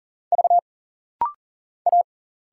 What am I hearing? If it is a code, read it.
Morse code, heard as a keyed tone, about 41 wpm, 700 Hz. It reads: V A
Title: Single Letter-Number Alt - Rapid-Fire 40wpm